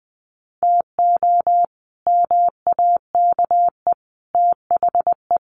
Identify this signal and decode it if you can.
Morse code, heard as a keyed tone, about 20 wpm, 700 Hz. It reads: TO MAKE THE
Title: Sets of 5 Words Encore2 20wpm